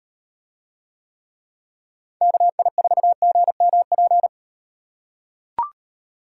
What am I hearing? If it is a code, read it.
Morse code, heard as a keyed tone, about 38 wpm, 700 Hz. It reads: KI4GMP